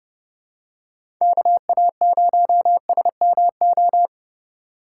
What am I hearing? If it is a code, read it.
Morse code, heard as a keyed tone, about 30 wpm, 700 Hz. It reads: KA0SMO
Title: USA Call Signs - 4x Character Spacing - 30wpm